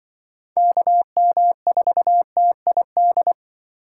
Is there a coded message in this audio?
Yes: Morse code from 0.6 to 3.3 s, heard as a keyed tone, about 25 wpm, 700 Hz. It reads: KM4TID